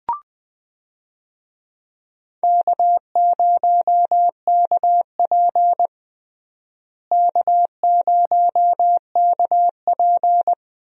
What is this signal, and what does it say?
Morse code, heard as a keyed tone, about 21 wpm, 700 Hz. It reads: K0KP K0KP
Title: USA Call Signs - 4 Characters - Repeated 3x - 20wpm